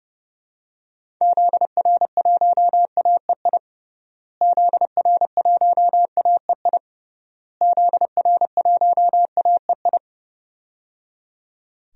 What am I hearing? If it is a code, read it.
Morse code, heard as a keyed tone, about 30 wpm, 700 Hz. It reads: ZR1AEI ZR1AEI ZR1AEI
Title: DXCC-100 Call Signs - 6 Characters - Repeated 3x - 30wpm